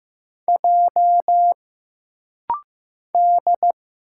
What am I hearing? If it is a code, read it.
Morse code, heard as a keyed tone, about 15 wpm, 700 Hz. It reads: J D
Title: Single Letter-Number Alt - Rapid-Fire 15wpm